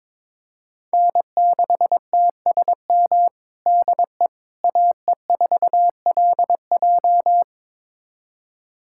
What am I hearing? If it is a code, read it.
Morse code, heard as a keyed tone, about 22 wpm, 700 Hz. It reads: N6TSM DE AE4LJ
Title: USA Callsign DE Callsign - 5 Characters - 22wpm